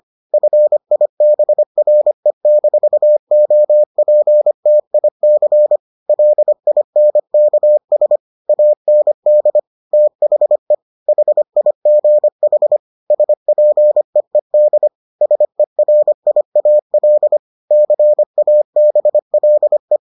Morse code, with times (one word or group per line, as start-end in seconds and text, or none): 0.3-5.8 FIBRE-OPTIC
6.1-8.2 LINKS
8.5-9.6 AND
9.9-10.7 THE
11.1-12.8 HIGH
13.1-14.9 SPEED
15.2-17.4 SERIAL
17.7-20.0 CABLE